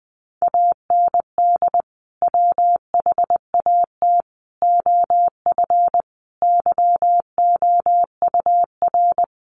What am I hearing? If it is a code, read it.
Morse code, heard as a keyed tone, about 20 wpm, 700 Hz. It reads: AND WHAT OF YOUR
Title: Sentences from Top 900 Words 20wpm